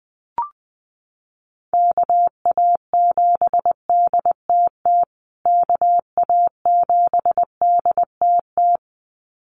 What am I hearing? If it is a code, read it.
Morse code, heard as a keyed tone, about 20 wpm, 700 Hz. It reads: KA7DTT KA7DTT
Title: USA Call Signs Repeated 2x 20wpm